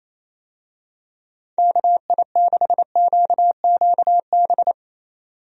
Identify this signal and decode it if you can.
Morse code, heard as a keyed tone, about 28 wpm, 700 Hz. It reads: KI6QQB
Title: USA Call Signs - 6 Characters - 28wpm